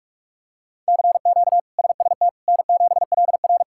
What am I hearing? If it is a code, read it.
Morse code, heard as a keyed tone, about 44 wpm, 700 Hz. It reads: CQ SST N7LR
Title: K1USN SST QSO Practice - 45wpm